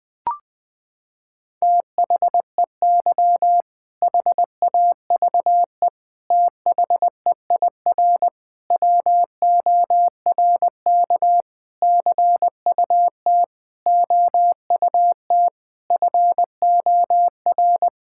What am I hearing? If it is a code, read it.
Morse code, heard as a keyed tone, about 20 wpm, 700 Hz. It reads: THEY HAVE THEIR WORK CUT OUT FOR
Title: Sentences with 7-Word Phrases - 20wpm